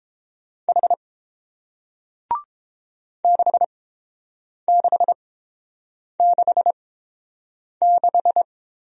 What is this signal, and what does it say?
Morse code, heard as a keyed tone, about 30 wpm, spaced as for 7 wpm, 700 Hz. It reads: H 6666